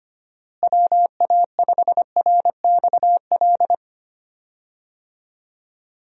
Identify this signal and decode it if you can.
Morse code, heard as a keyed tone, about 25 wpm, 700 Hz. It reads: WA5RXL
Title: USA Call Signs - 6 Characters - Repeated 3x - 25wpm